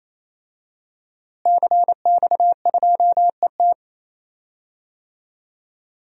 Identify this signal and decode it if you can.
Morse code, heard as a keyed tone, about 27 wpm, 700 Hz. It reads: CX2ET
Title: DXCC-100 Call Signs - 5 Characters - Repeated 3x - 28wpm